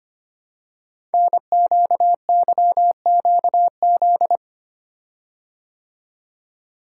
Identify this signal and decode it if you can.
Morse code, heard as a keyed tone, about 25 wpm, 700 Hz. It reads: NQYQZ
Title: Code Groups - 5 Letters - 25wpm